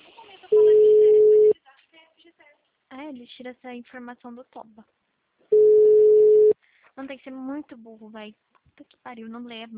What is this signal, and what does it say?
Ringback tone (Europe, 425 Hz), with speech.
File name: out- -2002- - - .9504